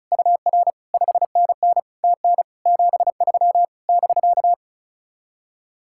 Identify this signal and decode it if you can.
Morse code, heard as a keyed tone, about 43 wpm, 700 Hz. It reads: UR 5NN TN 73 <BK>
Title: Speed-Racing Common QSO Phrases - 35wpm